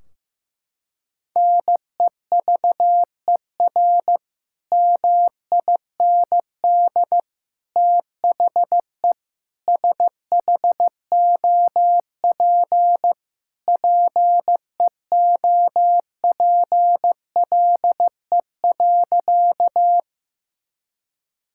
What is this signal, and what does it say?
Morse code, heard as a keyed tone, about 15 wpm, 700 Hz. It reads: NEVER MIND THE SHOP PEOPLE.
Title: Sentences from Top 1600 Words 15wpm